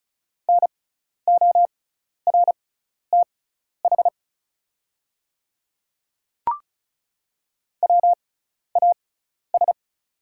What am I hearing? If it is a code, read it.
Morse code, heard as a keyed tone, about 36 wpm, spaced as for 12 wpm, 700 Hz. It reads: NORTH WAS
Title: 6x Character Spacing - Top 300 Words - 35wpm